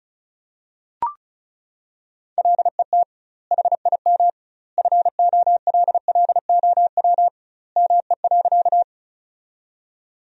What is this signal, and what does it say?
Morse code, heard as a keyed tone, about 36 wpm, 700 Hz. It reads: LET HIM FOLLOW ME.
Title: 2x Word-Spacing - Sentences from Top 300 Words - 35wpm